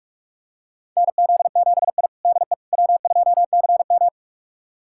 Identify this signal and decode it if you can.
Morse code, heard as a keyed tone, about 44 wpm, 690 Hz. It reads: NZ7I DE W2CM
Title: USA Callsign DE Callsign - 4 Characters - 45wpm